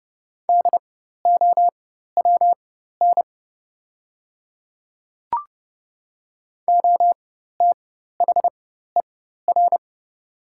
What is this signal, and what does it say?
Morse code, heard as a keyed tone, about 30 wpm, spaced as for 14 wpm, 700 Hz. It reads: DOWN OTHER
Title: 4x Character Spacing - Top 200 Words - 30wpm